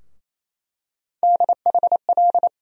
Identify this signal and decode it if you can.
Morse code, heard as a keyed tone, about 27 wpm, 700 Hz. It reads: DHL
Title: USA Call Sign Postfixes - 3 Letters 28wpm